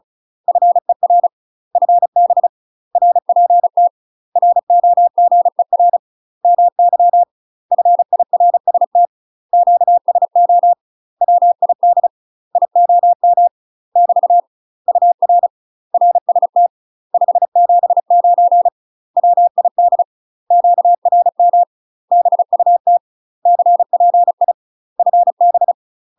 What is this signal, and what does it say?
Morse code, heard as a keyed tone, about 36 wpm, 700 Hz. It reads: FER FB RPT ROGER MY FIRST QSO WID IOM <BT> UR RST 579 WID QRM BUT CPI FB
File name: UK MASTER QSOs TRAINING / 35WPM / UK MASTER QSOs 0023